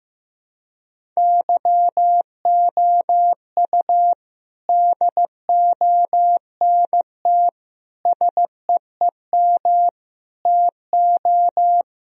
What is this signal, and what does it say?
Morse code, heard as a keyed tone, about 15 wpm, 700 Hz. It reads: YOU DONT SEEM TO
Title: Sentences from Top 1000 Words - Review Part 1 - 15wpm